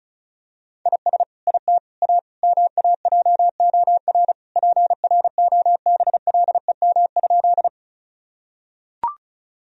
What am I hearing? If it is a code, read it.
Morse code, heard as a keyed tone, about 36 wpm, 700 Hz. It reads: IS IT A MAJOR PROBLEM?
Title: Sentences from Top 1000 Words - Review Part 4 - 35wpm